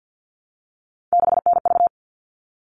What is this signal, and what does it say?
Morse code, heard as a keyed tone, about 50 wpm, 700 Hz. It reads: 6NV
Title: All Cut Numbers - 3-Digit 50wpm